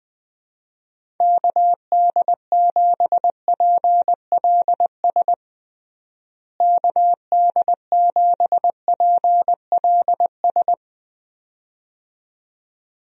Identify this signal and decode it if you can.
Morse code, heard as a keyed tone, about 21 wpm, 700 Hz. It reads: KD7PLS KD7PLS
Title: USA Call Signs - 6 Characters - Repeated 2x - 20wpm